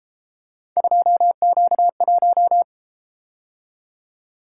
Morse code, heard as a keyed tone, about 31 wpm, 700 Hz. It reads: 2Q1